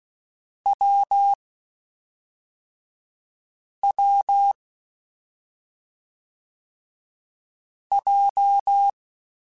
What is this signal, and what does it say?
Morse code, heard as a keyed tone, about 16 wpm, 800 Hz. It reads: W W J